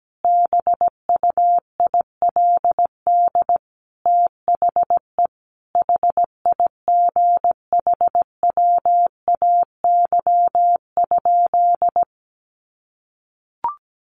Morse code, heard as a keyed tone, about 17 wpm, 700 Hz. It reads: BUILD THE HIGHWAY?
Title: Sentences from Top 1400 Words 17wpm